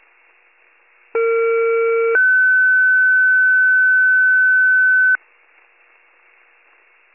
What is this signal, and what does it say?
Two-tone page, heard as a one-second tone, then a three-second tone, about 460 then 1500 Hz. It is heard from 1.1 to 5.2 s.